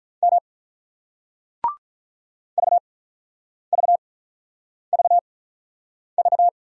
Morse code, heard as a keyed tone, about 50 wpm, spaced as for 9 wpm, 700 Hz. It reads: M VVVV